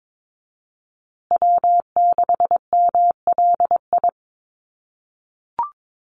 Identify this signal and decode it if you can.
Morse code, heard as a keyed tone, about 22 wpm, 700 Hz. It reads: W6MLI
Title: USA Call Signs Repeated 2x 22wpm